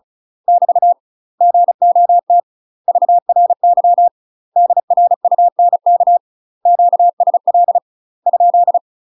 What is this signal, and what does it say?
Morse code, heard as a keyed tone, about 34 wpm, 700 Hz. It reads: <BT> GOT VRY DRUNK QSL ?
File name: UK MASTER QSOs TRAINING / 35WPM / UK MASTER QSOs 0012